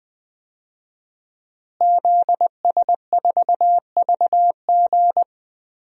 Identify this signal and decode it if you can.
Morse code, heard as a keyed tone, about 20 wpm, 700 Hz. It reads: ZS4VG
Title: DXCC-100 Call Signs - 5 Characters - Repeated 3x - 20wpm